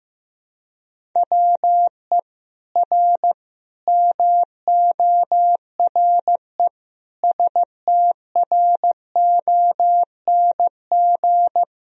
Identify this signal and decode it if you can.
Morse code, heard as a keyed tone, about 15 wpm, 700 Hz. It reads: WE R MORE STRONG
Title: Sentences from Top 500 Words and Abbreviations 15wpm